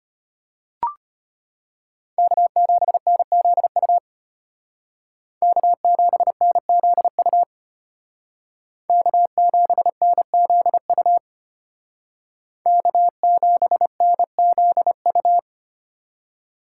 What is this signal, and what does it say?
Morse code, heard as a keyed tone, about 32 wpm, 700 Hz. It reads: K7NZU K7NZU K7NZU K7NZU